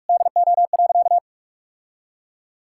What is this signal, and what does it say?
Morse code, heard as a keyed tone, about 46 wpm, 700 Hz. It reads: DO.
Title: Sentences from Top 1000 Words - Review Part 6 - 45wpm